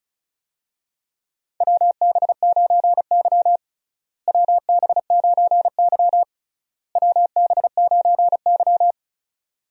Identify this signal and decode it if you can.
Morse code, heard as a keyed tone, about 34 wpm, 700 Hz. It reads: WB9Y WB9Y WB9Y
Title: USA Call Signs - 4 Characters - Repeated 3x - 35wpm